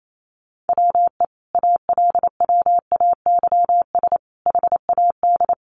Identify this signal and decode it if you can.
Morse code, heard as a keyed tone, about 28 wpm, 700 Hz. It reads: WE ALWAYS HAD
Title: Sentences from Top 1300 Words 28wpm